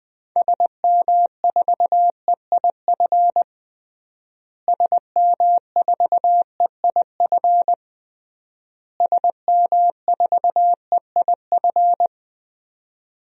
Morse code, heard as a keyed tone, about 20 wpm, 700 Hz. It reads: SM4EIF SM4EIF SM4EIF
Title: DXCC-100 Call Signs - 6 Characters - Repeated 3x - 20wpm